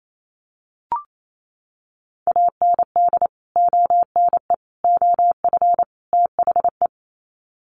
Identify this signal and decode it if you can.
Morse code, heard as a keyed tone, about 28 wpm, 700 Hz. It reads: AND ONE OF THE
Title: Sets of 4 Words 28wpm